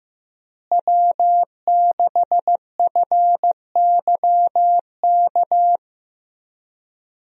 Morse code, heard as a keyed tone, about 15 wpm, 700 Hz. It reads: W6FYK
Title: USA Call Signs - 5 Characters - Repeated 2x - 15wpm